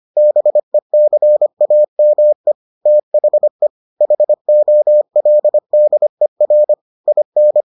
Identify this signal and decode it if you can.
Morse code, heard as a keyed tone, about 25 wpm, 600 Hz. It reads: BECAME THE HOLDER IN